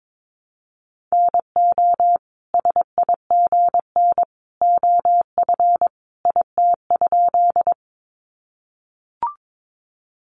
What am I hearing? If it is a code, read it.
Morse code, heard as a keyed tone, about 22 wpm, 700 Hz. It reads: NO SIGN OF IT?
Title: Sentences from Top 1000 Words 22wpm